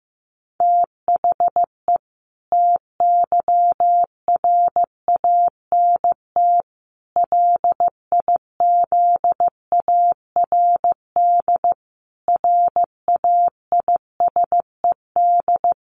Morse code, heard as a keyed tone, about 15 wpm, 700 Hz. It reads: THE TYRANT LIZARD RAISED
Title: A Sound Of Thunder 15wpm